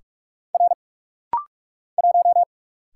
Morse code, heard as a keyed tone, about 44 wpm, 700 Hz. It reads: R 1
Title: Single Letter-Number Alt - Rapid-Fire 45wpm